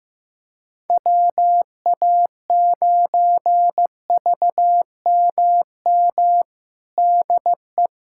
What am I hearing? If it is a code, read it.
Morse code, heard as a keyed tone, about 15 wpm, 700 Hz. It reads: WA9VMM DE
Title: USA Callsign DE Callsign - 6 Characters - 15wpm